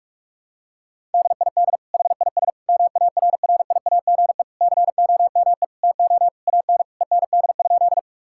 Morse code, heard as a keyed tone, about 41 wpm, 700 Hz. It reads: DID HIS MARRIAGE COME TO AN END?
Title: Sentences from Top 1000 Words - Review Part 5 - 45wpm